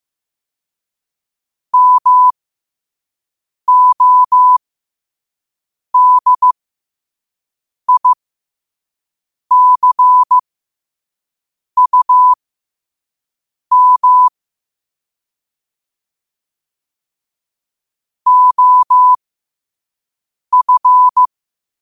Morse code, heard as a keyed tone, about 15 wpm, spaced as for 5 wpm, 1000 Hz. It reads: MODICUM OF
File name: qotdfiles / 5wpm